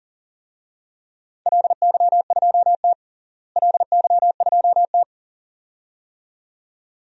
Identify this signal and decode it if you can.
Morse code, heard as a keyed tone, about 40 wpm, 700 Hz. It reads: LY2T LY2T
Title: DXCC-100 Call Signs - 4 Characters - Repeated 2x - 40wpm